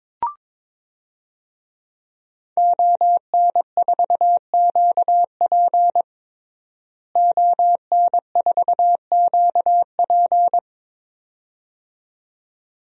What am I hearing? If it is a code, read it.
Morse code, heard as a keyed tone, about 21 wpm, 700 Hz. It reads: ON4QP ON4QP
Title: DXCC-100 Call Signs - 5 Characters - Repeated 2x - 22wpm